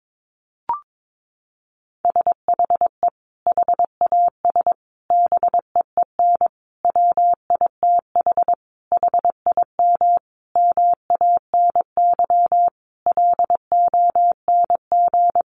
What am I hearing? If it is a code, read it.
Morse code, heard as a keyed tone, about 22 wpm, 700 Hz. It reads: SHE HAS BEEN WITH HIM MANY LONG
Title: Sentences from Top 200 Words 22wpm